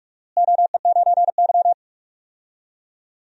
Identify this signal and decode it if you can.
Morse code, heard as a keyed tone, about 46 wpm, 700 Hz. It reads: OE9Y
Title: DXCC-100 Call Signs - 4 Characters - Repeated 2x - 45wpm